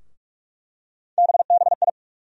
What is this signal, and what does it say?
Morse code, heard as a keyed tone, about 46 wpm, 700 Hz. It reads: BBI
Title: USA Call Sign Postfixes - 3 Letters 45wpm